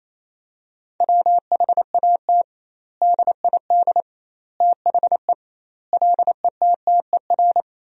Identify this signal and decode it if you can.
Morse code, heard as a keyed tone, about 27 wpm, 700 Hz. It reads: WHAT DID THE LETTER
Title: 2x Word-Spacing - Sentences from Top 500 Words - 28wpm